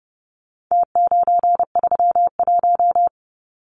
Morse code, heard as a keyed tone, about 30 wpm, 700 Hz. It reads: T931
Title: Numbers - Contest Style - 4-Digit 30wpm